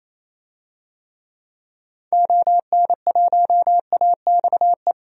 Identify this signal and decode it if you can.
Morse code, heard as a keyed tone, about 28 wpm, 700 Hz. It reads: ON1AXE
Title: DXCC-100 Call Signs - 6 Characters - 28wpm